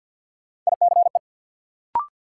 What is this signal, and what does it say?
Morse code, heard as a keyed tone, about 52 wpm, 700 Hz. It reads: ICE
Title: Sentences from Top 1000 Words - New Word Drill - 50wpm